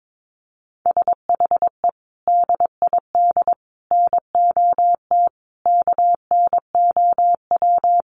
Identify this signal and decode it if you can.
Morse code, heard as a keyed tone, about 22 wpm, 700 Hz. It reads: SHE DID NOT KNOW
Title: Sets of 5 Words Encore2 22wpm